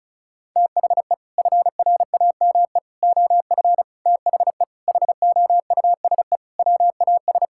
Morse code, heard as a keyed tone, about 34 wpm, 700 Hz. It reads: THE FRAME OF THE HOUSE WAS
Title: Sentences from Top 1000 Words - Review Part 6 - 35wpm